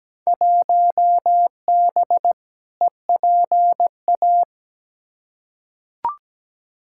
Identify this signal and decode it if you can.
Morse code, heard as a keyed tone, about 17 wpm, 700 Hz. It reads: 1B EPA
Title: Field Day Exchanges - 17wpm